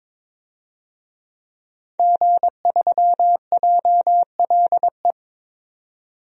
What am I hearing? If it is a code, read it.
Morse code, heard as a keyed tone, about 22 wpm, 700 Hz. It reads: G3JLE
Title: DXCC-100 Call Signs - 5 Characters - 22wpm